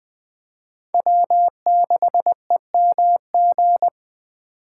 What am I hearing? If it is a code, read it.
Morse code, heard as a keyed tone, about 21 wpm, 700 Hz. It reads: W6EMG
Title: Speed Racing - USA 1x3 Call Signs 17wpm